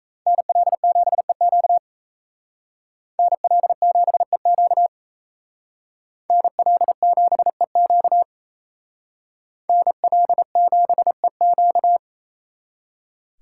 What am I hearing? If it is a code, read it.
Morse code, heard as a keyed tone, about 35 wpm, 700 Hz. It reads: NL7EQ NL7EQ NL7EQ NL7EQ